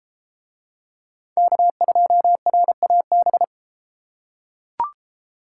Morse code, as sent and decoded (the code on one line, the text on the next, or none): -.- ..--- .-. .- -...
K2RAB